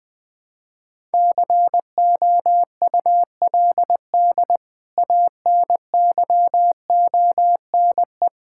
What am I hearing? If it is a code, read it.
Morse code, heard as a keyed tone, about 20 wpm, 700 Hz. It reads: COULD ANYONE
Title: Sentences from Top 1000 Words - Review Part 5 - 20wpm